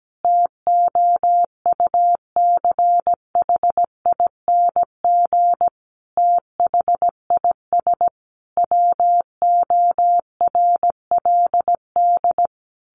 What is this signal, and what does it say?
Morse code, heard as a keyed tone, about 17 wpm, 700 Hz. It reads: TOUCHING THIS WORLD
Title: A Sound Of Thunder 17wpm